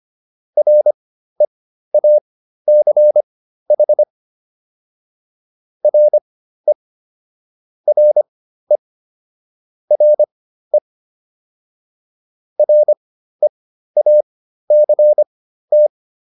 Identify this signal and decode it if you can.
Morse code, heard as a keyed tone, about 25 wpm, spaced as for 13 wpm, 600 Hz. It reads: REACH RE RE RE REACT